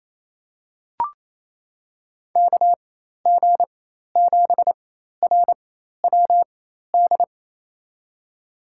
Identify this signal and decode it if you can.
Morse code, heard as a keyed tone, about 28 wpm, spaced as for 13 wpm, 700 Hz. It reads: KG7RWD
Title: USA Call Signs - 6 Characters - 4x Character Spacing - 28wpm